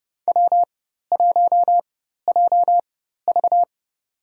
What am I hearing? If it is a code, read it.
Morse code, heard as a keyed tone, about 30 wpm, spaced as for 14 wpm, 700 Hz. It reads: W1JV
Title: USA Call Signs - 4 Characters - 4x Character Spacing - 30wpm